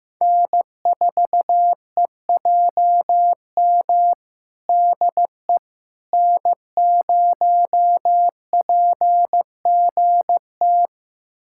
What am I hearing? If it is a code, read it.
Morse code, heard as a keyed tone, about 15 wpm, 700 Hz. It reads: N4EJM DE N0PGT
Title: USA Callsign DE Callsign - 5 Characters - 15wpm